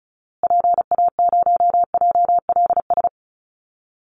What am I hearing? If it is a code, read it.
Morse code, heard as a keyed tone, about 34 wpm, 700 Hz. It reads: PA0JLS
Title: Call Signs Repeated 2x - 35wpm